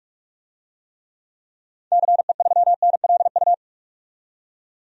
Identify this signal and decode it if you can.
Morse code, heard as a keyed tone, about 45 wpm, 700 Hz. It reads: CE3NLU
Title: DXCC-100 Call Signs - 6 Characters - 45wpm